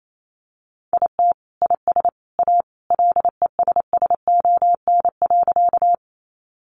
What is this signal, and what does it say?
Morse code, heard as a keyed tone, about 28 wpm, 700 Hz. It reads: IT IS A LESSON.
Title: Sentences from Top 1200 Words 28wpm